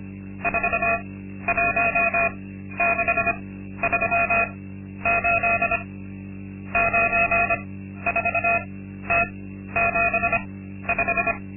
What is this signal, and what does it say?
Morse code, heard as a keyed tone, about 25 wpm, spaced as for 12 wpm, 690 Hz. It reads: 41638 94T75
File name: Linksmakalnis-2